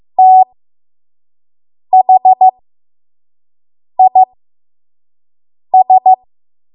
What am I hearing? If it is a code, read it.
Morse code, heard as a keyed tone, about 15 wpm, spaced as for 5 wpm, 750 Hz. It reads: THIS
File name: Morse / 05WPM